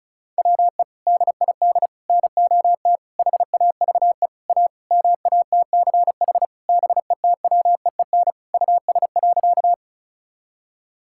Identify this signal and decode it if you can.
Morse code, heard as a keyed tone, about 36 wpm, 700 Hz. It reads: WE DID NOT HAVE A MATCH BETWEEN US.
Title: Sentences from Top 1600 Words 35wpm